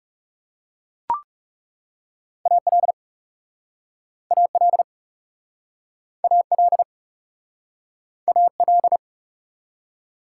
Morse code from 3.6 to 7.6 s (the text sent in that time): AL AL